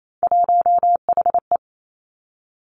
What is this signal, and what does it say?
Morse code, heard as a keyed tone, about 28 wpm, 690 Hz. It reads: 1HE
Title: Three Challenging Letter-Number Combinations 28wpm